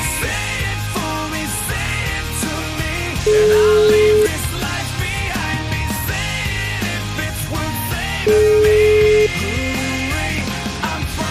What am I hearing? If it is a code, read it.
Ringback tone (Europe, 425 Hz), with noise.